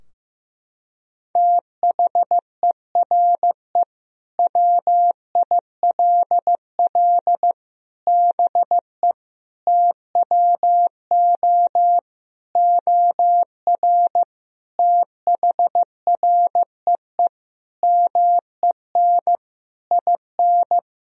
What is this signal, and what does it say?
Morse code, heard as a keyed tone, about 15 wpm, 700 Hz. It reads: THERE WILL BE TWO OR THREE MEN IN
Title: Sentences from Top 1000 Words - Review Part 3 - 15wpm